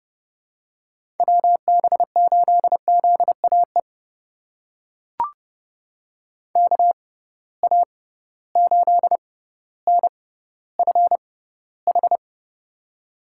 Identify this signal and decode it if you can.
Morse code, heard as a keyed tone, about 30 wpm, spaced as for 15 wpm, 700 Hz. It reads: WB8ZAE KA8NFH
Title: USA Call Signs - 6 Characters - 6x Character Spacing - 30wpm